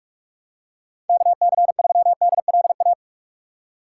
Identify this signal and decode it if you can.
Morse code, heard as a keyed tone, about 43 wpm, 700 Hz. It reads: KC3DLA
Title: USA Call Signs - 6 Characters - 2x Character Spacing - 45wpm